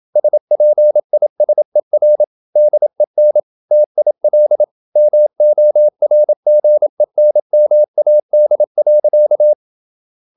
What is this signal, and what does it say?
Morse code, heard as a keyed tone, about 27 wpm, 600 Hz. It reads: SPISER DEN TIL MORGENMAD.
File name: Ferie QRQ 27wpm